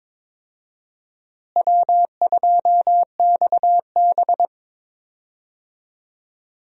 Morse code, heard as a keyed tone, about 22 wpm, 700 Hz. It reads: W2XB
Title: USA Call Signs - 4 Characters - Repeated 3x - 22wpm